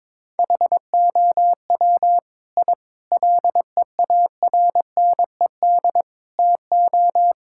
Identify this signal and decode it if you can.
Morse code, heard as a keyed tone, about 22 wpm, 700 Hz. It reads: HOW I LEARNED TO
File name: practice / 22wpm / Sets of 8 Words - 22wpm